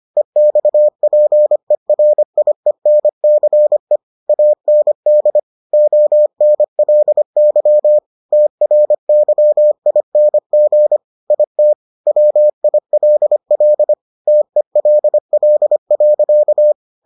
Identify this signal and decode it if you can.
Morse code, heard as a keyed tone, about 25 wpm, 600 Hz. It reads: EXPERIENCE AND ONLY TRYING IT WILL TELL.